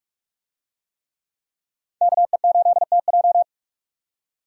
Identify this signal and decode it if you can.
Morse code, heard as a keyed tone, about 46 wpm, 700 Hz. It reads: KE8TJ
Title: USA Call Signs - 5 Characters - 45wpm